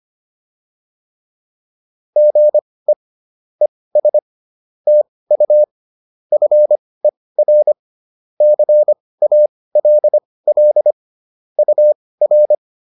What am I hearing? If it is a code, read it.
Morse code, heard as a keyed tone, about 25 wpm, spaced as for 18 wpm, 600 Hz. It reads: GE ES TU FER CALL UR
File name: QSO102 18